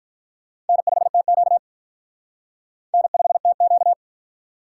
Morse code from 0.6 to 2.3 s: N5TQ

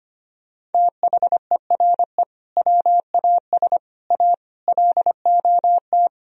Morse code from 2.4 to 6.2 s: WAS A LOT

